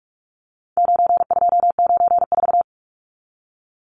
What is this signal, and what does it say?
Morse code, heard as a keyed tone, about 44 wpm, 700 Hz. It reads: 9294
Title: Numbers - 4-Digit 45wpm